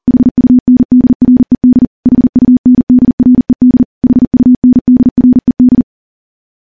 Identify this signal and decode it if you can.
Morse code, heard as a keyed tone, about 40 wpm, 250 Hz. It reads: HUNDRED HUNDRED HUNDRED
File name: morse / morse-training-40WPM